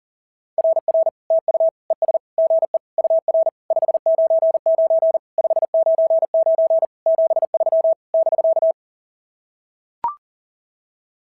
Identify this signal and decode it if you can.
Morse code, heard as a keyed tone, about 39 wpm, 690 Hz. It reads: RR TU ES GE UR 599 599 73 <BK>
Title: SOTA QSO Practice 40wpm